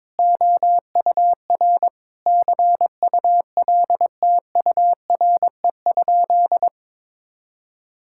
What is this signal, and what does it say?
Morse code, heard as a keyed tone, about 21 wpm, 700 Hz. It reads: OUR CULTURE?
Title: Sentences from Top 1000 Words - Review Part 6 - 22wpm